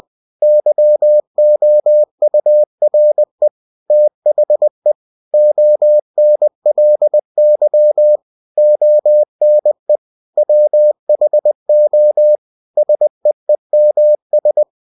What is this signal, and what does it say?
Morse code, heard as a keyed tone, about 20 wpm, 600 Hz. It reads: YOURE THE ONLY ONE WHO SEEMS
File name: cw-books / winnie-the-pooh / winnie-20wpm-page-0088